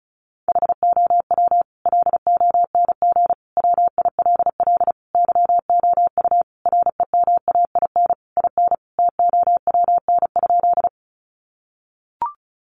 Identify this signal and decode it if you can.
Morse code, heard as a keyed tone, about 34 wpm, 700 Hz. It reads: HOW LONG WILL YOU REMAIN IN TOWN?
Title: Sentences from Top 1200 Words 35wpm